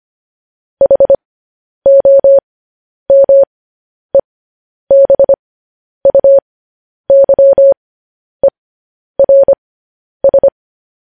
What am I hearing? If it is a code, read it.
Morse code, heard as a keyed tone, about 25 wpm, spaced as for 10 wpm, 550 Hz. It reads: HOMEBUYERS
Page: news headlines in morse code at 10 wpm